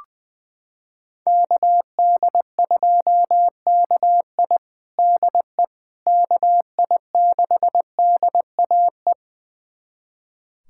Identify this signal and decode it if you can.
Morse code, heard as a keyed tone, about 21 wpm, 700 Hz. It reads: KD2KI DE KI6DAE
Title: USA Callsign DE Callsign 20wpm